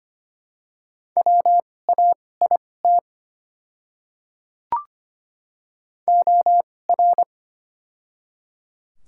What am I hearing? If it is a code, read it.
Morse code, heard as a keyed tone, about 25 wpm, spaced as for 18 wpm, 700 Hz. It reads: WAIT OR
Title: 2x Character Spacing - Top 500 Words - 25wpm